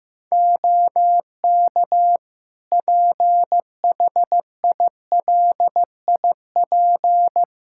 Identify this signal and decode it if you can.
Morse code, heard as a keyed tone, about 16 wpm, 700 Hz. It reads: OK PHILIP
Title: K1USN SST QSO Practice 15wpm